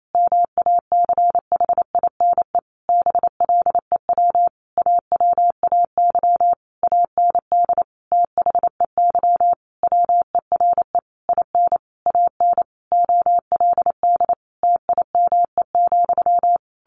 Morse code, heard as a keyed tone, about 28 wpm, 700 Hz. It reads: MACHINE BLEW AWAY AND THEY WERE IN AN OLD TIME,
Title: A Sound Of Thunder 28wpm